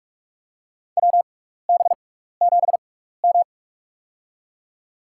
Morse code, heard as a keyed tone, about 48 wpm, spaced as for 16 wpm, 700 Hz. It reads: WB7M
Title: USA Call Signs - 4 Characters - 6x Character Spacing - 45wpm